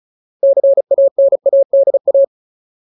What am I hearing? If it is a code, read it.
Morse code, heard as a keyed tone, about 34 wpm, 550 Hz. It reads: CANADA
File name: Ultimate Word List / 35WPM Ultimate